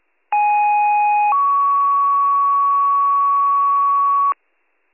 Two-tone page, heard as a one-second tone, then a three-second tone, about 830 then 1120 Hz. It is heard from 0.3 to 4.3 s.